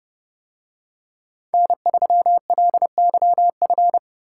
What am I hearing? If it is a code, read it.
Morse code, heard as a keyed tone, about 29 wpm, 700 Hz. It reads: N3LYF